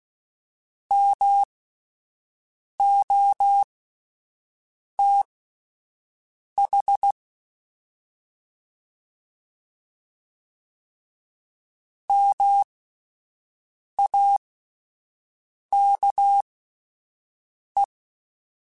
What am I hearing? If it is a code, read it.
Morse code, heard as a keyed tone, about 16 wpm, spaced as for 5 wpm, 800 Hz. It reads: MOTH MAKE